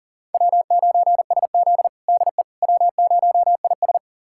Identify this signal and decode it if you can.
Morse code, heard as a keyed tone, about 39 wpm, 700 Hz. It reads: W9SZ DE W0IS
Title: USA Callsign DE Callsign - 4 Characters - 40wpm